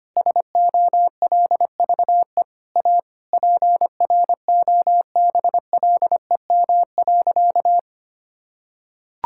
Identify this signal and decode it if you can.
Morse code, heard as a keyed tone, about 25 wpm, 700 Hz. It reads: SOLVE A PROBLEM.
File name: practice / 25wpm / Sentences with 9-Word Phrases - 25wpm